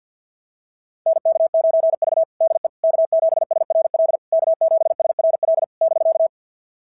Morse code, heard as a keyed tone, about 47 wpm, 690 Hz. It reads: NK9V DE K7SRL K7SRL <BK>
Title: SOTA QSO Practice 50wpm